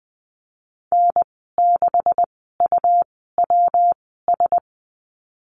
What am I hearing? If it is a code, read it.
Morse code, heard as a keyed tone, about 20 wpm, spaced as for 15 wpm, 700 Hz. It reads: N6UWS
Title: 154 Lesson 25g - Letter p - Call Signs 20wpm